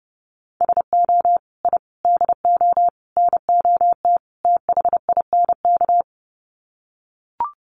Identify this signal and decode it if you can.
Morse code, heard as a keyed tone, about 30 wpm, 700 Hz. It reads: SO I DO NOT THINK
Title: Sets of 5 Words 30wpm